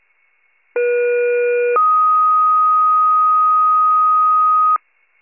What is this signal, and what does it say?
Two-tone page, heard as a one-second tone, then a three-second tone, about 480 then 1250 Hz. It is heard from 0.7 to 4.8 s.